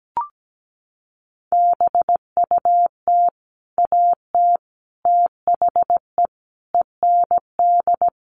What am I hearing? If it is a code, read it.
Morse code, heard as a keyed tone, about 17 wpm, 700 Hz. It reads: BUT AT THE END
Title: Sets of 5 Words Encore2 17wpm